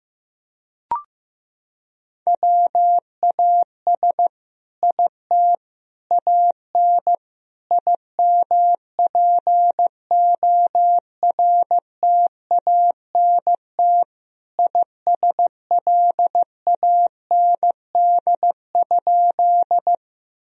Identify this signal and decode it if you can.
Morse code, heard as a keyed tone, about 15 wpm, 700 Hz. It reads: WAS IT AN IMPORTANT ISLAND?
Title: Sentences from Top 1000 Words - Review Part 3 - 15wpm